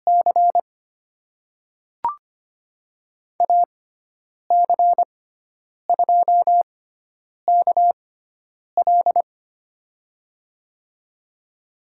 Morse code, heard as a keyed tone, about 25 wpm, spaced as for 9 wpm, 700 Hz. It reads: C AC2KL
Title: USA Call Signs - 5 Characters - 6x Character Spacing - 25wpm